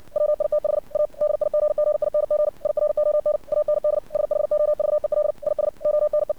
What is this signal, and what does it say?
Morse code, heard as a keyed tone, about 57 wpm, 600 Hz. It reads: WITH A DIGGING IRON AND SHOVEL IS ONE
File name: archived / CRSnet